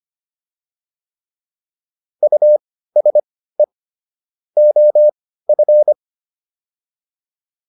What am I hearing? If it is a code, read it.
Morse code, heard as a keyed tone, about 25 wpm, spaced as for 15 wpm, 600 Hz. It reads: USE OF